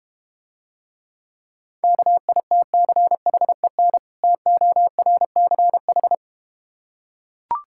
Morse code, heard as a keyed tone, about 32 wpm, 690 Hz. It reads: KITCHEN TORCH